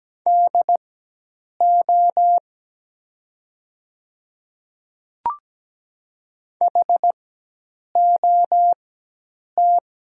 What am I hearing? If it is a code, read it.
Morse code, heard as a keyed tone, about 17 wpm, spaced as for 8 wpm, 700 Hz. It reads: DO HOT
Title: 4x Character Spacing - Top 100 Words - 17wpm